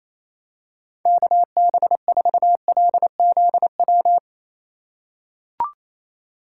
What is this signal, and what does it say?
Morse code, heard as a keyed tone, about 28 wpm, 700 Hz. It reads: KB4LZW
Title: USA Call Signs - 2x Character Spacing - 28wpm